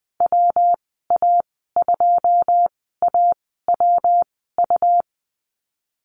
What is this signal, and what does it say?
Morse code, heard as a keyed tone, about 20 wpm, spaced as for 14 wpm, 700 Hz. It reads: WA2AWU
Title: 231 Lesson 37g - Letter x - Call Signs 20wpm